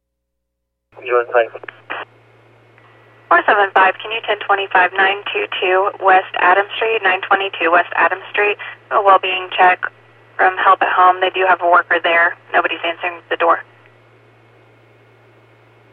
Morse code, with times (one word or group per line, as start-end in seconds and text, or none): none